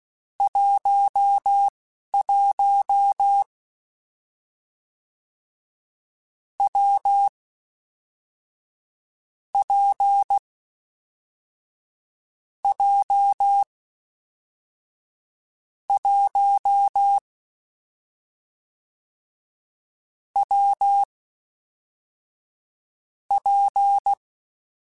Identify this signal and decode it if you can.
Morse code, heard as a keyed tone, about 16 wpm, spaced as for 11 wpm, 800 Hz. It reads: 11 W P J 1 W P
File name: Difficult Combos / Difficult Combos-mp3s / WPJ WPJ1 @10